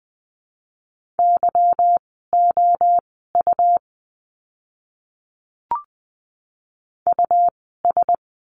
Morse code, heard as a keyed tone, about 20 wpm, spaced as for 14 wpm, 700 Hz. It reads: YOU US
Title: 192 Lesson 31f - Letter v - Words 20wpm